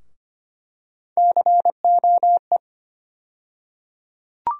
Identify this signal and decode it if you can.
Morse code, heard as a keyed tone, about 25 wpm, 700 Hz. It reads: COE
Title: USA Call Sign Postfixes - 3 Letters 25wpm